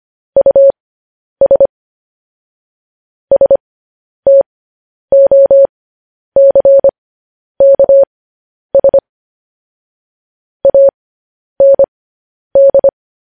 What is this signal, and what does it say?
Morse code, heard as a keyed tone, about 25 wpm, spaced as for 10 wpm, 550 Hz. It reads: US STOCKS AND